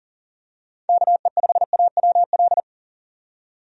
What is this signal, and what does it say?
Morse code, heard as a keyed tone, about 40 wpm, 700 Hz. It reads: KE5AWL
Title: USA Call Signs - 6 Characters - 40wpm